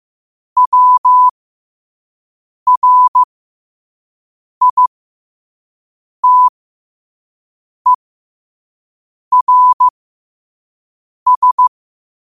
Morse code, heard as a keyed tone, about 15 wpm, spaced as for 5 wpm, 1000 Hz. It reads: WRITERS
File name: qotdfiles / 5wpm